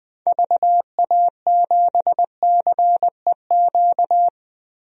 Morse code, heard as a keyed tone, about 20 wpm, 700 Hz. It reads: VA7CEQ
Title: DXCC-100 Call Signs - 6 Characters - 20wpm